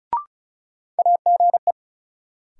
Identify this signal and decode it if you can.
Morse code, heard as a keyed tone, about 36 wpm, 700 Hz. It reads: AGE
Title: Sets of 3 Words - New Word Drill - 35wpm